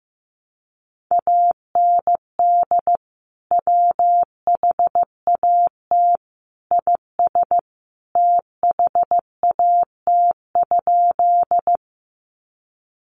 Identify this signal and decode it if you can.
Morse code, heard as a keyed tone, about 14 wpm, 700 Hz. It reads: AND WHAT IS THAT?
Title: Sentences from Top 100 Words 15wpm